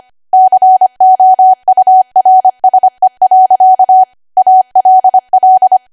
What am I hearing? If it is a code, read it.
Morse code, heard as a keyed tone, about 25 wpm, 750 Hz. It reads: COURSE. ALL
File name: Archive / 25 WPM / 25WPM